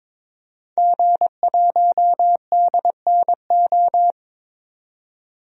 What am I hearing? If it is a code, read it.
Morse code, heard as a keyed tone, about 22 wpm, 700 Hz. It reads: G1DNO
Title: DXCC-100 Call Signs - 5 Characters - 22wpm